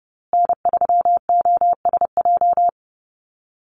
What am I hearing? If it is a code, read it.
Morse code, heard as a keyed tone, about 30 wpm, 700 Hz. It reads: N3OSJ
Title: USA Call Signs Repeated 2x 30wpm